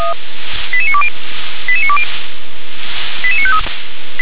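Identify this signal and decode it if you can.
Five-tone selective call (ZVEI), heard as quick runs of five tones, tones of about 70 ms. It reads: 88920 88920 88953
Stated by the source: Five-tone ZVEI selective call received on an SDR.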